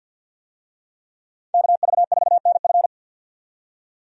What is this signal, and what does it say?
Morse code, heard as a keyed tone, about 50 wpm, 700 Hz. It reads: KV4NF
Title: USA Call Signs - 5 Characters - 50wpm